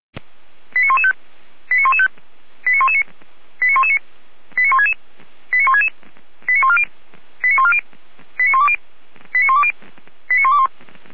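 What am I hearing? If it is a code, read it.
Five-tone selective call (ZVEI), heard as quick runs of five tones, tones of about 70 ms. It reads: 89115 89115 89119 89119 89166 89166 89140 89140 89120 89120 89121
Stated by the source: Received on an SDR; ZVEI five-tone call 89072.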